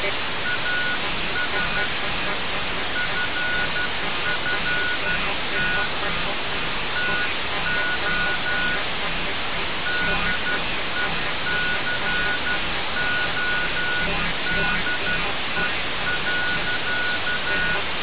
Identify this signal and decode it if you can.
Morse code, heard as a keyed tone, about 12 wpm, spaced as for 11 wpm, 1460 Hz, with noise. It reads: AS FFT TO DECODE.
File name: radiocap / REC z